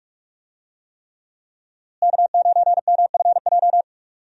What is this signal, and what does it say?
Morse code, heard as a keyed tone, about 46 wpm, 700 Hz. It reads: K9MFJ
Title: USA Call Signs - 5 Characters - Repeated 3x - 45wpm